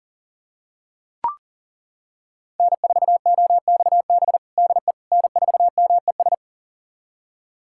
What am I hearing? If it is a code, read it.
Morse code, heard as a keyed tone, about 40 wpm, 700 Hz. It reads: N4OXB DE N4MES
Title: USA Callsign DE Callsign - 5 Characters - 40wpm